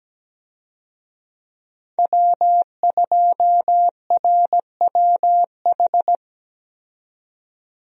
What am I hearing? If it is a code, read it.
Morse code, heard as a keyed tone, about 17 wpm, 700 Hz. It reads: W2RWH
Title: USA Call Signs - 5 Characters - 17wpm